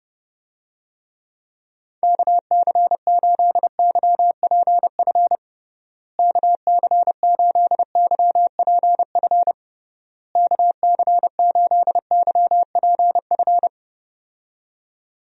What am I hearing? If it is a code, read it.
Morse code, heard as a keyed tone, about 32 wpm, 700 Hz. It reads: KC8YPF KC8YPF KC8YPF
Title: USA Call Signs - 6 Characters - Repeated 3x - 30wpm